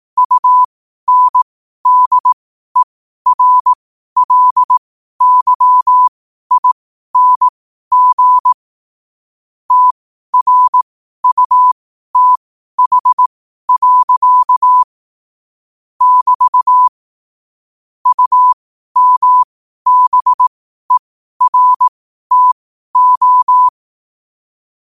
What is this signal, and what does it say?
Morse code, heard as a keyed tone, about 18 wpm, spaced as for 12 wpm, 1000 Hz. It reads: UNDERLYING TRUTH. <BT> UMBERTO
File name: qotdfiles / 12wpm